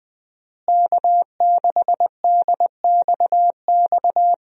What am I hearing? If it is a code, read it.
Morse code, heard as a keyed tone, about 20 wpm, 700 Hz. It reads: K6DXX
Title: USA Call Signs - 5 Characters - Repeated 3x - 20wpm